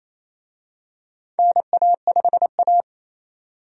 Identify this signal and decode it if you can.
Morse code, heard as a keyed tone, about 28 wpm, 700 Hz. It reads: NA5A
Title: USA Call Signs - 4 Characters - 28wpm